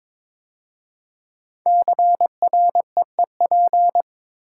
Morse code, heard as a keyed tone, about 22 wpm, 700 Hz. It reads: CREEP
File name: 5 Letter Words - First 3 Letters Given - 22wpm